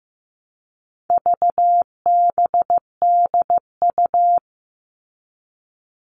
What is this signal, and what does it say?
Morse code, heard as a keyed tone, about 15 wpm, 700 Hz. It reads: VBDU
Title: All Cut Numbers - 4-Digit 15wpm